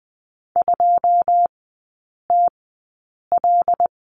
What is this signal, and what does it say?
Morse code, heard as a keyed tone, about 20 wpm, spaced as for 8 wpm, 690 Hz. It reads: 2TL
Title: 145 Lesson 24e - Letter y - Three Characters 20wpm